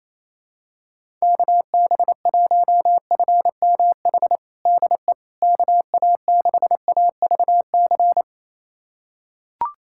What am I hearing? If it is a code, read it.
Morse code, heard as a keyed tone, about 28 wpm, 700 Hz. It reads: KB1FMH DE KA6AVC
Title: USA Callsign DE Callsign - 6 Characters - 28wpm